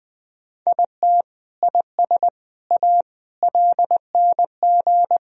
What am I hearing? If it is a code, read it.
Morse code, heard as a keyed tone, about 20 wpm, 700 Hz. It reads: IT IS A LNG
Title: Sentences from Top 300 Words and Abbreviations 20wpm